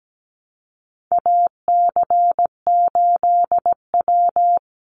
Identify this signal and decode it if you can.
Morse code, heard as a keyed tone, about 17 wpm, 700 Hz. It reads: AC8W
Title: Call Signs Repeated 2x - 17wpm